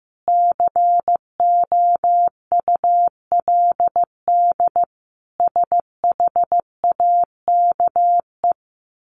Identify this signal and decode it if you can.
Morse code, heard as a keyed tone, about 15 wpm, 700 Hz. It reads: COULD SHAKE
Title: A Sound Of Thunder 15wpm